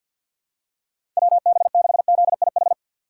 Morse code, heard as a keyed tone, about 47 wpm, 700 Hz. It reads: WB6ZIH
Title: USA Call Signs - 6 Characters - 3x Character Spacing - 50wpm